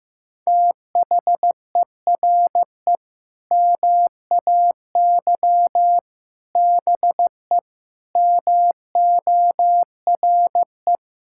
Morse code, heard as a keyed tone, about 15 wpm, 700 Hz. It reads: THERE MAY BE MORE
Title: Sentences from Top 1000 Words - Review Part 6 - 15wpm